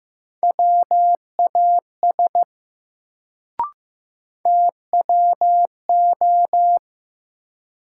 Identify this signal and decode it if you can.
Morse code, heard as a keyed tone, about 15 wpm, 700 Hz. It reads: WAS TWO
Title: Sentences from Top 100 Words - New Word Drill - 15wpm